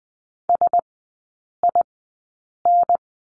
Morse code, heard as a keyed tone, about 20 wpm, spaced as for 8 wpm, 700 Hz. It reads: SIN
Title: 041 Lesson 8e - Number 1 - Three Characters 20wpm